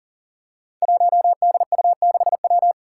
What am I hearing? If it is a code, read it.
Morse code, heard as a keyed tone, about 40 wpm, 690 Hz. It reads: 1DU6W
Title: Five Challenging Letter-Number Combinations 40wpm